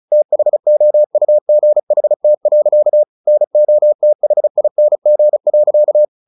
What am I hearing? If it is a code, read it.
Morse code, heard as a keyed tone, about 34 wpm, 600 Hz. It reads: THOUGHT. NOTHING.
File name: the old man and the sea 2 35wpm